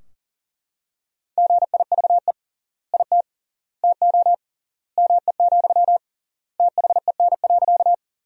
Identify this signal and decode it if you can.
Morse code, heard as a keyed tone, about 39 wpm, 700 Hz. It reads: GIVE IT TO ME, THEN.
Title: 3x Word-Spacing - Sentences from Top 200 Words - 40wpm